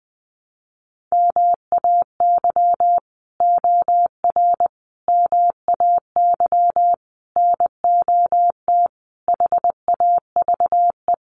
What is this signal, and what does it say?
Morse code, heard as a keyed tone, about 19 wpm, 700 Hz. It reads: MAY OR MAY NOT HAVE
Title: Sets of 5 Words Encore2 20wpm